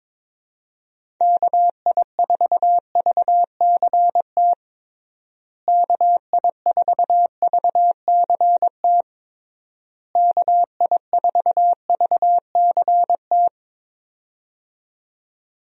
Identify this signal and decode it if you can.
Morse code, heard as a keyed tone, about 22 wpm, 700 Hz. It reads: KI4VCT KI4VCT KI4VCT
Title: USA Call Signs - 6 Characters - Repeated 3x - 22wpm